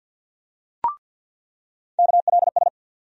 Morse code, heard as a keyed tone, about 47 wpm, 700 Hz. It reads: KLS
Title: USA Call Sign Postfixes - 3 Letters 50wpm